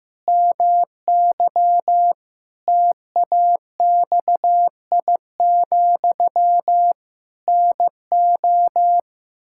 Morse code, heard as a keyed tone, about 15 wpm, 700 Hz. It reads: MY TAXI, NO
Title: Pangram Challenge v2 - 15wpm